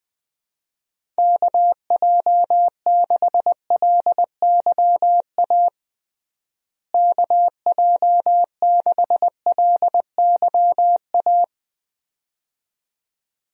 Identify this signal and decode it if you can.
Morse code, heard as a keyed tone, about 20 wpm, 700 Hz. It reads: KJ6LYA KJ6LYA
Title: USA Call Signs - 6 Characters - Repeated 2x - 20wpm